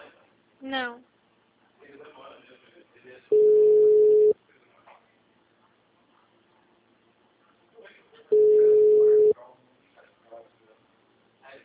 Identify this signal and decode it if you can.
Ringback tone (Europe, 425 Hz), with speech.